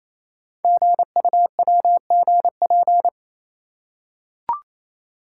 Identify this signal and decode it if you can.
Morse code, heard as a keyed tone, about 27 wpm, 700 Hz. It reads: GUWGP
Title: Code Groups - 5 Letters - 28wpm